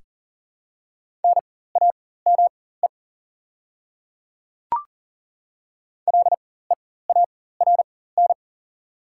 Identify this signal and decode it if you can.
Morse code, heard as a keyed tone, about 39 wpm, spaced as for 19 wpm, 700 Hz. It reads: NAME LEARN
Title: 4x Character Spacing - Top 500 Words - 40wpm